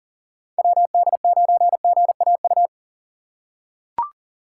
Morse code, heard as a keyed tone, about 38 wpm, 700 Hz. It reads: WD9GAU
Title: USA Call Signs - 6 Characters - 6x Character Spacing - 40wpm